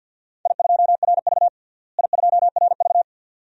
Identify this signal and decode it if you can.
Morse code, heard as a keyed tone, about 47 wpm, 700 Hz. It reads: I2RV I2RV
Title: DXCC-100 Call Signs - 4 Characters - Repeated 2x - 50wpm